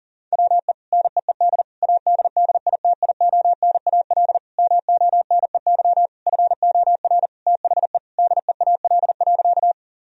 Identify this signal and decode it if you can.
Morse code, heard as a keyed tone, about 40 wpm, 700 Hz. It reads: WE NEED ADDITIONAL MONEY FOR THE DEAL.
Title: Sentences from Top 1000 Words - Review Part 4 - 40wpm